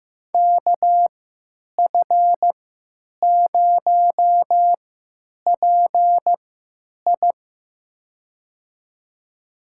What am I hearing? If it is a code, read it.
Morse code, heard as a keyed tone, about 15 wpm, spaced as for 9 wpm, 700 Hz. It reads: KF0PI
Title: USA Call Signs - 5 Characters - 3x Character Spacing - 15wpm